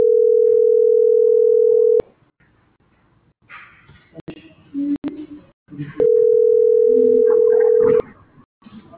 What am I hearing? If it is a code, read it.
Ringback tone (North America), 440 + 480 Hz.